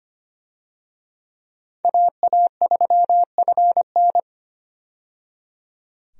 Morse code, heard as a keyed tone, about 25 wpm, 700 Hz. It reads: AA3FN
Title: USA Call Signs - 5 Characters - 25wpm